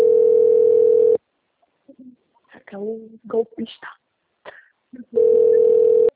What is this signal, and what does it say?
Ringback tone (North America), 440 + 480 Hz, with speech.
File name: out- - - - - .21274